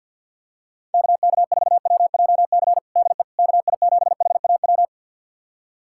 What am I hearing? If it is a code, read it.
Morse code, heard as a keyed tone, about 51 wpm, 700 Hz. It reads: KK4WJC DE KI7HAW
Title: USA Callsign DE Callsign - 6 Characters - 50wpm